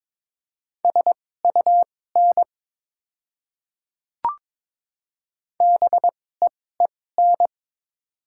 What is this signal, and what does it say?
Morse code, heard as a keyed tone, about 21 wpm, spaced as for 16 wpm, 700 Hz. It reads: SUN BEEN
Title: 2x Character Spacing - Top 300 Words - 22wpm